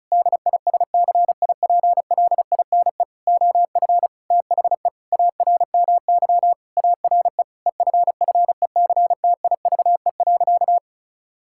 Morse code, heard as a keyed tone, about 36 wpm, 700 Hz. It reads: DISCIPLINE OF THE ARMY ARE EFFECTIVE.
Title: Sentences from Top 1000 Words - Review Part 6 - 35wpm